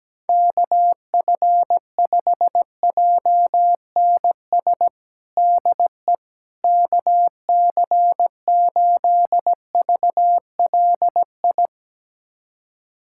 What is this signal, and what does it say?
Morse code, heard as a keyed tone, about 17 wpm, 700 Hz. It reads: KF5JNS DE KC8VLI
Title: USA Callsign DE Callsign - 6 Characters - 17wpm